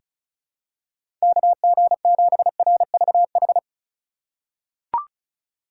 Morse code, heard as a keyed tone, about 36 wpm, 700 Hz. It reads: KG7RVH
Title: USA Call Signs - 6 Characters - 3x Character Spacing - 35wpm